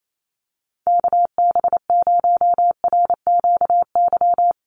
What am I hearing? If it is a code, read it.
Morse code, heard as a keyed tone, about 28 wpm, 700 Hz. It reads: KB0RQY
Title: USA Call Signs 28wpm